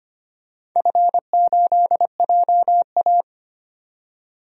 Morse code, heard as a keyed tone, about 25 wpm, 700 Hz. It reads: F8JA
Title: DXCC-100 Call Signs - 4 Characters - 25wpm